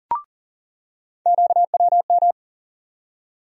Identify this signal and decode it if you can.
Morse code, heard as a keyed tone, about 40 wpm, 700 Hz. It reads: QWM
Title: USA Call Sign Postfixes - 3 Letters 40wpm